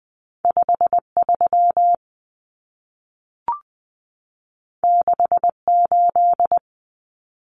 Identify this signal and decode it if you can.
Morse code, heard as a keyed tone, about 20 wpm, 700 Hz. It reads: 53 68
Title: Common Cut Numbers - 2-Digit 20wpm